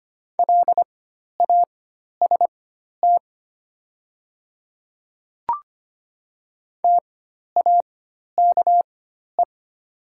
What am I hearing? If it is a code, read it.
Morse code, heard as a keyed tone, about 25 wpm, spaced as for 12 wpm, 690 Hz. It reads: LAST TAKE
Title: 4x Character Spacing - Top 300 Words - 25wpm